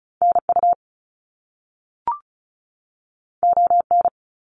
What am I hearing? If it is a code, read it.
Morse code, heard as a keyed tone, about 36 wpm, 700 Hz. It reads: NU ON
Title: Canadian Provinces and Territories 35wpm